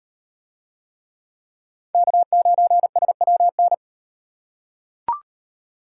Morse code, heard as a keyed tone, about 38 wpm, 700 Hz. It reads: K9SWN